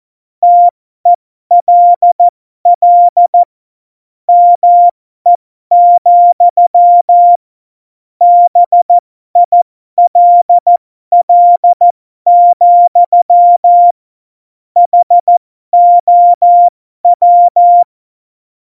Morse code, heard as a keyed tone, about 14 wpm, spaced as for 12 wpm, 700 Hz. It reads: TELL ME, BILL, HOW
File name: Morsekurs / Witze